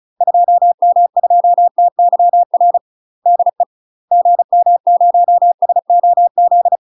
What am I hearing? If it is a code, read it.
Morse code, heard as a keyed tone, about 34 wpm, 700 Hz. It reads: 2M2TYR DE GM0SOZ
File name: UK MASTER QSOs TRAINING / 35WPM / UK MASTER QSOs 0001